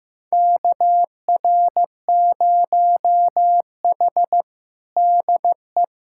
Morse code, heard as a keyed tone, about 15 wpm, 700 Hz. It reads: KR0H DE
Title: USA Callsign DE Callsign - 4 Characters - 15wpm